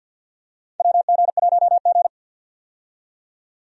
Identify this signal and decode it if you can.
Morse code, heard as a keyed tone, about 52 wpm, 700 Hz. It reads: WG1G
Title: USA Call Signs - 4 Characters - 50wpm